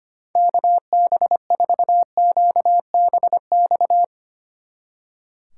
Morse code, heard as a keyed tone, about 29 wpm, 700 Hz. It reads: KB4QBX
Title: Speed Racing - USA 2x3 Call Signs 25wpm